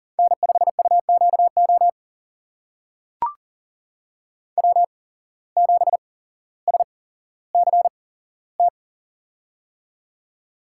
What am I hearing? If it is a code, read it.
Morse code, heard as a keyed tone, about 40 wpm, spaced as for 17 wpm, 700 Hz. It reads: N5UQO W7SCT
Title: USA Call Signs - 8x Character Spacing - 40wpm